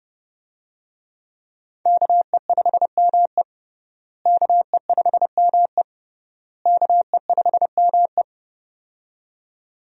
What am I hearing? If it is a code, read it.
Morse code, heard as a keyed tone, about 30 wpm, 700 Hz. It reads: KE5ME KE5ME KE5ME
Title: USA Call Signs - 5 Characters - Repeated 3x - 30wpm